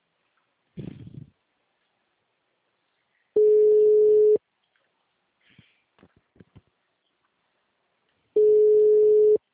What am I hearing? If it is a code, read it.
Ringback tone (Europe, 425 Hz).